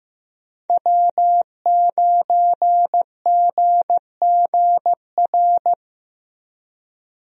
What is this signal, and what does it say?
Morse code, heard as a keyed tone, about 15 wpm, 700 Hz. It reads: W9GGR
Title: USA Call Signs - 5 Characters - 15wpm